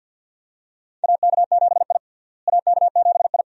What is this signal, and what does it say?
Morse code, heard as a keyed tone, about 48 wpm, 700 Hz. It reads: AK7I AK7I
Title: USA Call Signs - 4 Characters - Repeated 2x - 50wpm